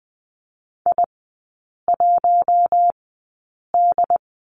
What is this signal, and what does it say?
Morse code, heard as a keyed tone, about 20 wpm, spaced as for 8 wpm, 690 Hz. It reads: I1D
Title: 191 Lesson 31e - Letter v - Three Characters 20wpm